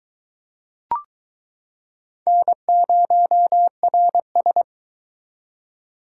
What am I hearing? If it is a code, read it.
Morse code, heard as a keyed tone, about 23 wpm, 700 Hz. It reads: N0RS